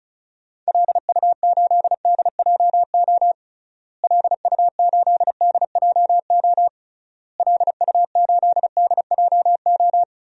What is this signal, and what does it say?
Morse code, heard as a keyed tone, about 36 wpm, 700 Hz. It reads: LU8DJO LU8DJO LU8DJO
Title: DXCC-100 Call Signs - 6 Characters - Repeated 3x - 35wpm